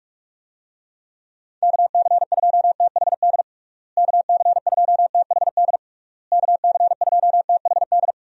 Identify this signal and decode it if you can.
Morse code, heard as a keyed tone, about 41 wpm, 700 Hz. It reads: KC2THD KC2THD KC2THD
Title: USA Call Signs - 6 Characters - Repeated 3x - 45wpm